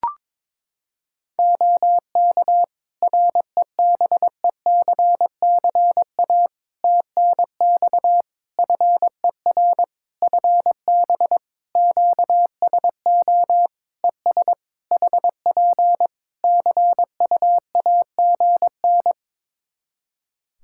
Morse code, heard as a keyed tone, about 22 wpm, 700 Hz. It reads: OK REBECCA TNX FER FB QSO ES HP CUAGN
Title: Practice QSO - Exchange 4 - Ending - 22wpm